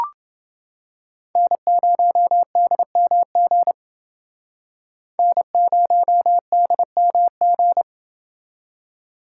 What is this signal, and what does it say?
Morse code, heard as a keyed tone, about 29 wpm, 700 Hz. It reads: N0DMG N0DMG